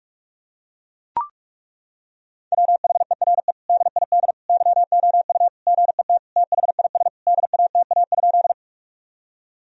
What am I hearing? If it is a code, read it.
Morse code, heard as a keyed tone, about 41 wpm, 700 Hz. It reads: WHERE DID YOU GET THIS DATA?
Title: Sentences from Top 1000 Words - Review Part 3 - 45wpm